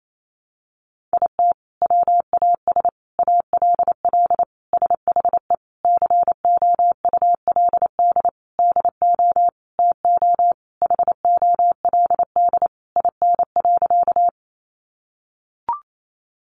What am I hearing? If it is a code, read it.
Morse code, heard as a keyed tone, about 28 wpm, 700 Hz. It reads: IT WAS ALL SHE COULD DO TO HOLD IN.
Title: Sentences from Top 400 Words 28wpm